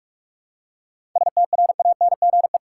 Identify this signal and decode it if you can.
Morse code, heard as a keyed tone, about 44 wpm, 700 Hz. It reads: STRANGE
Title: Seven Letter Words - 45wpm